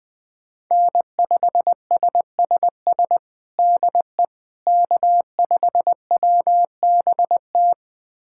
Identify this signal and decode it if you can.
Morse code, heard as a keyed tone, about 20 wpm, 700 Hz. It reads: N5SSS DE K5WBT
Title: USA Callsign DE Callsign - 5 Characters - 20wpm